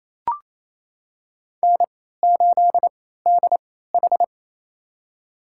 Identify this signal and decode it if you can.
Morse code, heard as a keyed tone, about 28 wpm, spaced as for 16 wpm, 700 Hz. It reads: N8DH
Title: USA Call Signs - 3x Character Spacing - 28wpm